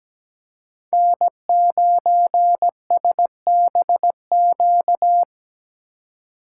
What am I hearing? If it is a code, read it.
Morse code, heard as a keyed tone, about 17 wpm, 700 Hz. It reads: N9SBQ
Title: USA Call Signs - 2x Character Spacing - 17wpm